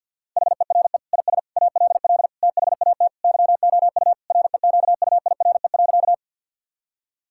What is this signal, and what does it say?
Morse code, heard as a keyed tone, about 50 wpm, 700 Hz. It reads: HERE IS ALL THAT YOU REQUIRE.
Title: Sentences from Top 1000 Words - Review Part 6 - 50wpm